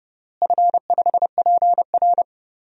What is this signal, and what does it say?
Morse code, heard as a keyed tone, about 30 wpm, 700 Hz. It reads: F5PR
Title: DXCC-100 Call Signs - 4 Characters - Repeated 3x - 30wpm